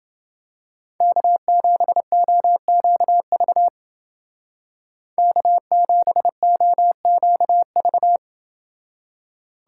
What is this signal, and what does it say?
Morse code, heard as a keyed tone, about 30 wpm, 700 Hz. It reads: K7OQV K7OQV